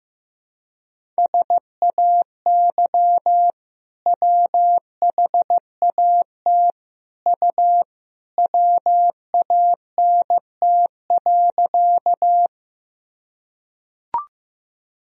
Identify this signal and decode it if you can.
Morse code, heard as a keyed tone, about 15 wpm, 700 Hz. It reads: SAY WHAT U WANT.
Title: Sentences from Top 200 Words and Abbreviations 15wpm